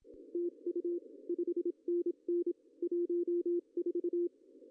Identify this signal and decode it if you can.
Morse code, heard as a keyed tone, about 25 wpm, 340 Hz. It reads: TU 5NN 14
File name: CW / cqww-cw-k5tr